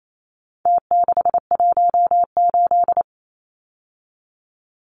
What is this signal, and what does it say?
Morse code, heard as a keyed tone, about 28 wpm, 700 Hz. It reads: T618
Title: Numbers - Contest Style - 4-Digit 28wpm